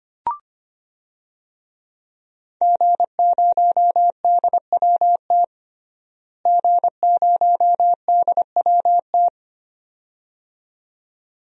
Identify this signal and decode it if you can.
Morse code, heard as a keyed tone, about 25 wpm, 700 Hz. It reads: G0DWT G0DWT
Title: DXCC-100 Call Signs - 5 Characters - Repeated 2x - 25wpm